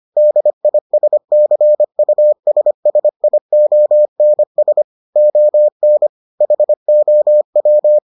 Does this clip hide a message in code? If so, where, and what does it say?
Yes: Morse code from 0.2 to 8.0 s, heard as a keyed tone, about 25 wpm, 600 Hz. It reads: DISCUSSIONS ON HOW